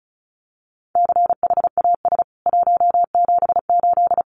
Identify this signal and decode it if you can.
Morse code, heard as a keyed tone, about 34 wpm, 700 Hz. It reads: CHAS 178
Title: CWT Exchange 35wpm